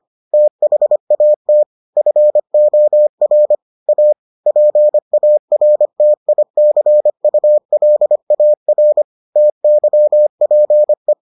Morse code, heard as a keyed tone, about 25 wpm, 600 Hz. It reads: THAT FOR A PARTICULAR TYPE